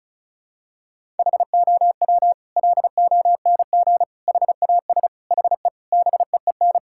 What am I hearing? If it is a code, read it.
Morse code, heard as a keyed tone, about 36 wpm, 700 Hz. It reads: HOW LONG HAS HE BEEN
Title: Sentences from Top 1000 Words - Review Part 6 - 35wpm